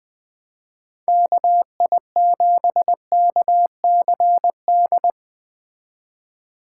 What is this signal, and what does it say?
Morse code, heard as a keyed tone, about 21 wpm, 700 Hz. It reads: KI7KCD